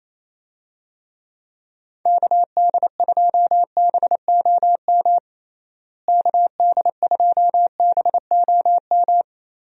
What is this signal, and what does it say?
Morse code, heard as a keyed tone, about 28 wpm, 700 Hz. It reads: KD2BOM KD2BOM
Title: USA Call Signs - 6 Characters - Repeated 3x - 28wpm